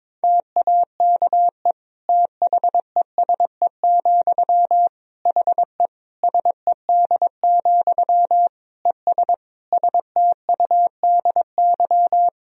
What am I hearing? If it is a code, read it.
Morse code, heard as a keyed tone, about 22 wpm, 700 Hz. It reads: TAKE THESE, HE SED, ES STUDY
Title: Sentences from Top 300 Words and Abbreviations 22wpm